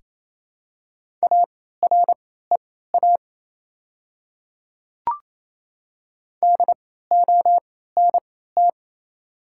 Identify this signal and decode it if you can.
Morse code, heard as a keyed tone, about 28 wpm, spaced as for 16 wpm, 690 Hz. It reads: AREA DONT
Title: 3x Character Spacing - Top 500 Words - 28wpm